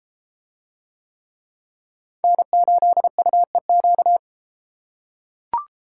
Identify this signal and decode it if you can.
Morse code, heard as a keyed tone, about 31 wpm, 700 Hz. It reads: N8UEQ